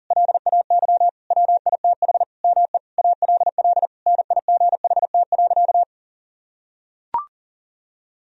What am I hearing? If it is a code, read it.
Morse code, heard as a keyed tone, about 40 wpm, 700 Hz. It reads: LAY WITH ME ALL NIGHT.
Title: Sentences from Top 1000 Words - Review Part 2 - 40wpm